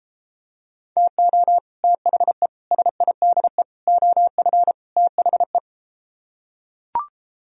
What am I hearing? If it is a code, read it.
Morse code, heard as a keyed tone, about 33 wpm, 700 Hz. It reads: TO THE SIDE OF THE